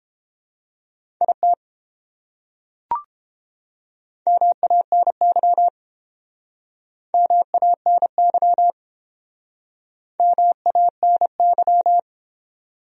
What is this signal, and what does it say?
Morse code, heard as a keyed tone, about 30 wpm, 700 Hz. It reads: IT MANY MANY MANY